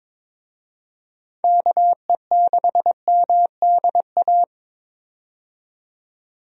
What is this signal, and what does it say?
Morse code, heard as a keyed tone, about 22 wpm, 700 Hz. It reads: KE6MDA
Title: USA Call Signs - 6 Characters - 22wpm